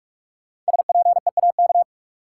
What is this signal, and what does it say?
Morse code, heard as a keyed tone, about 46 wpm, 700 Hz. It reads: SPEAK
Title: Sentences from Top 1100 Words - New Word Drill Alt - 45wpm